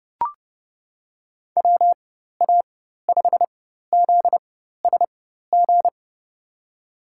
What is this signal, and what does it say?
Morse code, heard as a keyed tone, about 30 wpm, spaced as for 14 wpm, 700 Hz. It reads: WA5ZSG
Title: USA Call Signs - 6 Characters - 4x Character Spacing - 30wpm